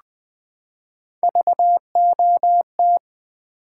Morse code, heard as a keyed tone, about 20 wpm, 700 Hz. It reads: VOT
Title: Build A Word - 3 Letters At A Time - 6-Letter Words - 20wpm